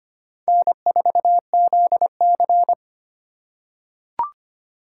Morse code, heard as a keyed tone, about 25 wpm, 700 Hz. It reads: N4ZC
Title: USA Call Signs - 2x Character Spacing - 25wpm